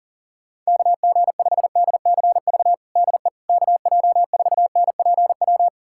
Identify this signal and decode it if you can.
Morse code, heard as a keyed tone, about 40 wpm, 700 Hz. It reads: KG5DCV DE KJ4NPW
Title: USA Callsign DE Callsign - 6 Characters - 40wpm